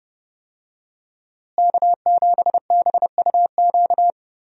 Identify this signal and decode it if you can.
Morse code, heard as a keyed tone, about 32 wpm, 700 Hz. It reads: K7BUQ
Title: USA Call Signs - 5 Characters - Repeated 2x - 30wpm